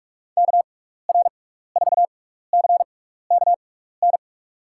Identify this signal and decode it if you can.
Morse code, heard as a keyed tone, about 45 wpm, spaced as for 15 wpm, 700 Hz. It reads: KR4CKN
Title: USA Call Signs - 6 Characters - 6x Character Spacing - 45wpm